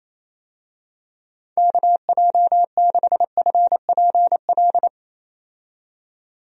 Morse code, heard as a keyed tone, about 27 wpm, 700 Hz. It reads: KJ6FPL
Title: USA Call Signs - 6 Characters - 28wpm